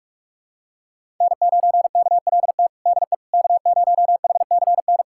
Morse code, heard as a keyed tone, about 46 wpm, 700 Hz. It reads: N9KLT DE K0HCN
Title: USA Callsign DE Callsign - 5 Characters - 45wpm